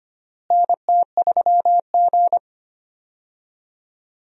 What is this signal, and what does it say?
Morse code, heard as a keyed tone, about 25 wpm, 700 Hz. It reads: NT3G
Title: USA Call Signs - 4 Characters - Repeated 3x - 25wpm